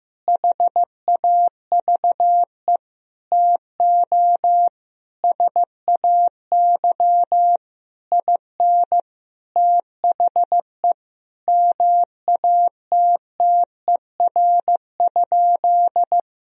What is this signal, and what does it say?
Morse code, heard as a keyed tone, about 15 wpm, 700 Hz. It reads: HAVE TO SAY IN THE MATTER?
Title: Sentences from Top 1000 Words - Review Part 3 - 15wpm